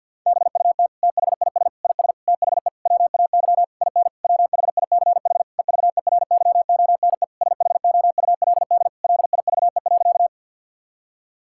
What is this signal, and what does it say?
Morse code, heard as a keyed tone, about 50 wpm, 700 Hz. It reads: BUT THIS IS THE WAY IN WHICH EVERYONE SHOULD LIVE.
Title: Sentences from Top 1000 Words - Review Part 5 - 50wpm